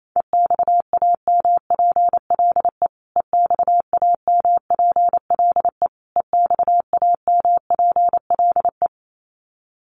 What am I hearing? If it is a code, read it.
Morse code, heard as a keyed tone, about 28 wpm, 700 Hz. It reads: EXAMPLE EXAMPLE EXAMPLE
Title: Top 500 Words Repeated 28wpm